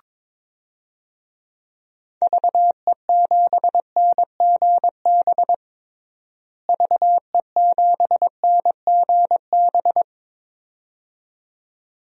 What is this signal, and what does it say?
Morse code, heard as a keyed tone, about 22 wpm, 700 Hz. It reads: VE7NGB VE7NGB
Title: DXCC-100 Call Signs - 6 Characters - Repeated 2x - 22wpm